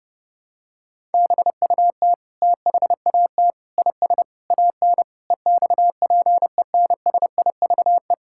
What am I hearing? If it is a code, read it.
Morse code, heard as a keyed tone, about 30 wpm, 700 Hz. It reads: BUT THAT IS AN EXPENSIVE
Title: Sentences from Top 1000 Words - Review Part 6 - 30wpm